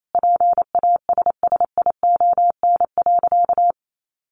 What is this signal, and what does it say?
Morse code, heard as a keyed tone, about 28 wpm, 700 Hz. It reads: PASSION.
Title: Sentences from Top 1300 Words 28wpm